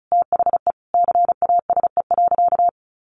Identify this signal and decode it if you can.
Morse code, heard as a keyed tone, about 34 wpm, 700 Hz. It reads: THE CASE.
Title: Sentences from Top 1200 Words 35wpm